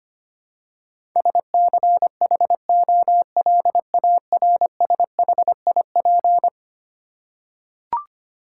Morse code, heard as a keyed tone, about 25 wpm, 700 Hz. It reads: SCHOLARSHIP
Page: morse-code-ninja-practice-25wpm